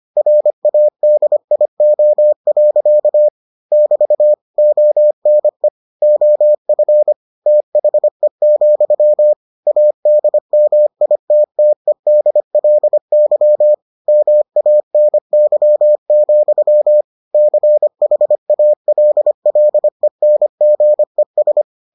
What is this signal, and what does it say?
Morse code, heard as a keyed tone, about 25 wpm, 600 Hz. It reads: RADIO. <BT> ONE OF THE, ADMITTEDLY MANY, CHALLENGES